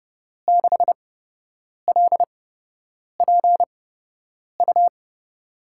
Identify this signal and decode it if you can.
Morse code, heard as a keyed tone, about 30 wpm, spaced as for 8 wpm, 700 Hz. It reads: 6LPU
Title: USA Call Signs - 5 Characters - 8x Character Spacing - 30wpm